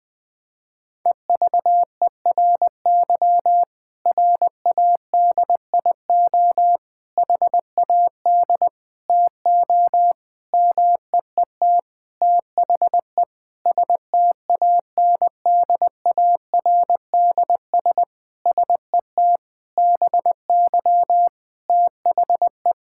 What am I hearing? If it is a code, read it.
Morse code, heard as a keyed tone, about 20 wpm, 700 Hz. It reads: EVERY RADIO HAD TO MEET THE STANDARDS SET BY THE
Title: Sentences from Top 1000 Words - Review Part 6 - 20wpm